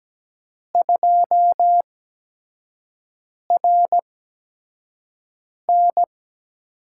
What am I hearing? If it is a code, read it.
Morse code, heard as a keyed tone, about 17 wpm, spaced as for 5 wpm, 700 Hz. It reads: 2RN
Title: USA Call Signs - 8x Character Spacing - 17wpm